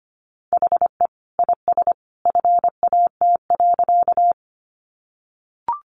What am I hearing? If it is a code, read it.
Morse code, heard as a keyed tone, about 25 wpm, 700 Hz. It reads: HE IS FAT.
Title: Easy Phrases 25wpm